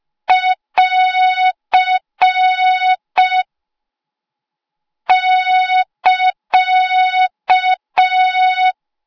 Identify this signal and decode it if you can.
Morse code, heard as a keyed tone, about 5 wpm, 3700 Hz. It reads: <AR> <KA>